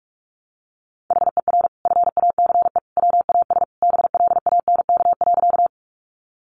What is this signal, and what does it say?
Morse code, heard as a keyed tone, about 44 wpm, 700 Hz. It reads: HER FACE WAS BLANK.
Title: Sentences from Top 1500 Words 45wpm